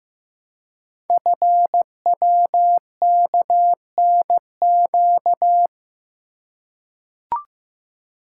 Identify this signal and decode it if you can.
Morse code, heard as a keyed tone, about 15 wpm, 700 Hz. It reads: FWKNQ
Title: Code Groups - 5 Letters - 15wpm